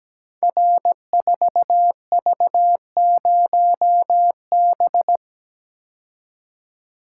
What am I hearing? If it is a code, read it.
Morse code, heard as a keyed tone, about 17 wpm, 700 Hz. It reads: R4V0B
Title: Code Groups - 5 Letter:Numbers - 17wpm